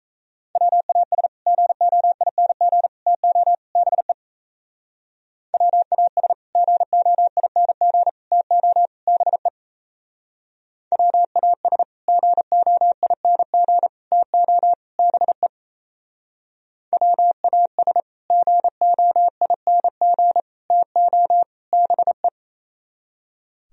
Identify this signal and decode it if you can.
Morse code, heard as a keyed tone, about 35 wpm, 700 Hz. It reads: WAS GOING TO BE WAS GOING TO BE WAS GOING TO BE WAS GOING TO BE
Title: Speed-Racing - Sets of 4 Words 28wpm